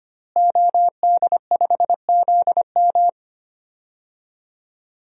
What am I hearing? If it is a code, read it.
Morse code, heard as a keyed tone, about 25 wpm, 700 Hz. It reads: OD5ZM
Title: DXCC-100 Call Signs - 5 Characters - Repeated 3x - 25wpm